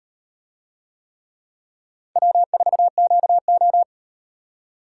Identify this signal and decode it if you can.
Morse code, heard as a keyed tone, about 36 wpm, 700 Hz. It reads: W4QO